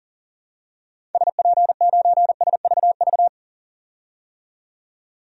Morse code, heard as a keyed tone, about 40 wpm, 700 Hz. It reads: SP9SVV
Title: DXCC-100 Call Signs - 6 Characters - 40wpm